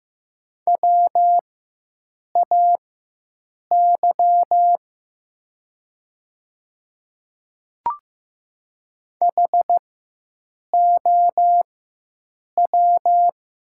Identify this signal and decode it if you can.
Morse code, heard as a keyed tone, about 15 wpm, spaced as for 7 wpm, 700 Hz. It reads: WAY HOW
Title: 4x Character Spacing - Top 100 Words - 15wpm